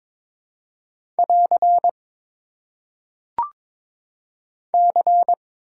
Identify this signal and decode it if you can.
Morse code, heard as a keyed tone, about 21 wpm, 700 Hz. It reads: <AR> C
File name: Top 90 QSO Elements - 4x Character Spacing - 22wpm